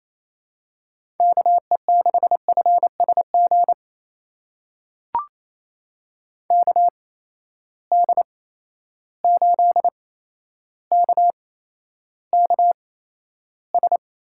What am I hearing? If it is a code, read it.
Morse code, heard as a keyed tone, about 28 wpm, spaced as for 12 wpm, 700 Hz. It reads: KE6FSG KD8KKS
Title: USA Call Signs - 8x Character Spacing - 28wpm